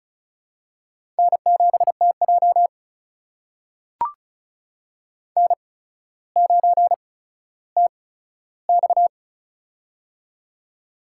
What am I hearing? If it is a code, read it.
Morse code, heard as a keyed tone, about 36 wpm, spaced as for 15 wpm, 700 Hz. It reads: N7TJ N9TX
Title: USA Call Signs - 4 Characters - 8x Character Spacing - 35wpm